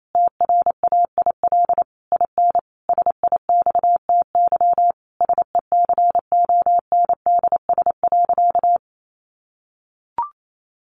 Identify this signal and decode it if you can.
Morse code, heard as a keyed tone, about 28 wpm, 700 Hz. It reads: TRAIL IN SIXTY SECONDS.
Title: A Sound Of Thunder 28wpm